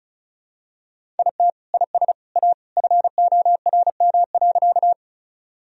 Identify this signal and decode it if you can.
Morse code, heard as a keyed tone, about 34 wpm, 700 Hz. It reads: IT IS A FORM.
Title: Sentences from Top 1000 Words - Review Part 1 - 35wpm